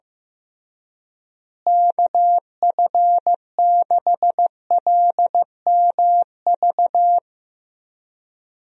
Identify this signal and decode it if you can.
Morse code, heard as a keyed tone, about 15 wpm, 700 Hz. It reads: KF6LMV
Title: USA Call Signs - 6 Characters - Repeated 3x - 15wpm